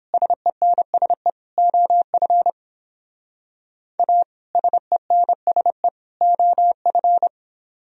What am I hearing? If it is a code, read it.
Morse code, heard as a keyed tone, about 31 wpm, 700 Hz. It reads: SENSE OF A SENSE OF